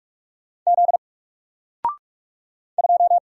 Morse code, heard as a keyed tone, about 44 wpm, 700 Hz. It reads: Z 2
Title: Single Character (Letter:Number:Punctuation:Prosign) Alt 45wpm